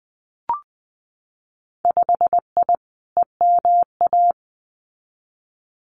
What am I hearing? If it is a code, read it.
Morse code, heard as a keyed tone, about 20 wpm, 700 Hz. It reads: 5I EMA
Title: Winter Field Day Exchanges Encore 20wpm